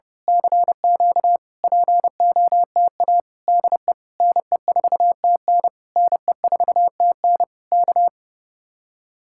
Morse code, heard as a keyed tone, about 29 wpm, 700 Hz. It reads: CQ POTA DE NE4TN NE4TN K
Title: Speed-Racing Common QSO Phrases Encore - 30wpm